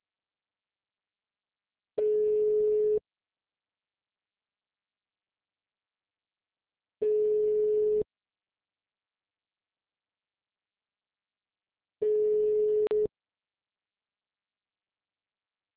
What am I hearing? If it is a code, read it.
Ringback tone (Europe, 425 Hz).
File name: out- -2003- - - .15344